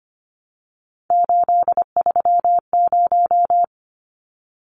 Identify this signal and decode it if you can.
Morse code, heard as a keyed tone, about 25 wpm, 700 Hz. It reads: 830
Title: Numbers - 3-Digit 25wpm